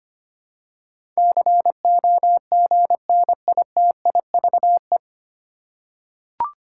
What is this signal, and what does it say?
Morse code, heard as a keyed tone, about 25 wpm, 700 Hz. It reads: COGNITIVE
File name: practice / 25wpm / 9 Letter Words - First 3 Letters Given - 25wpm